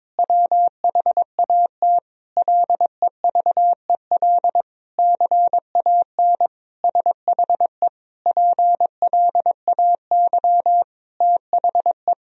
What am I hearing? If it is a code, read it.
Morse code, heard as a keyed tone, about 22 wpm, 700 Hz. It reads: WHAT LEVEL CAN SHE PLAY THE
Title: Sentences from Top 1000 Words - Review Part 4 - 22wpm